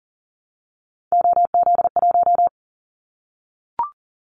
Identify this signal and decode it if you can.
Morse code, heard as a keyed tone, about 40 wpm, 700 Hz. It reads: OZ1
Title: Callsign Prefixes for Top 100 DXCC - 40wpm